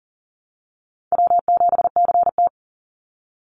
Morse code, heard as a keyed tone, about 40 wpm, 700 Hz. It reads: W7CT
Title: Easy Call Signs - 40wpm